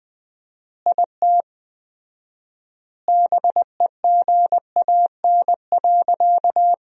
Morse code, heard as a keyed tone, about 21 wpm, 700 Hz. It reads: IT BEGAN.
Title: 4x Word-Spacing - Sentences from Top 300 Words - 20wpm